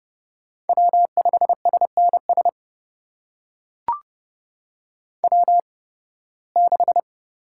Morse code, heard as a keyed tone, about 30 wpm, spaced as for 20 wpm, 700 Hz. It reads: W5SNS W6
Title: USA Call Signs - 5 Characters - 8x Character Spacing - 30wpm